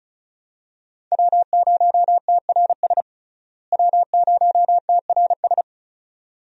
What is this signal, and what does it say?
Morse code, heard as a keyed tone, about 34 wpm, 700 Hz. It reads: W0TRS W0TRS
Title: USA Call Signs - 5 Characters - Repeated 2x - 35wpm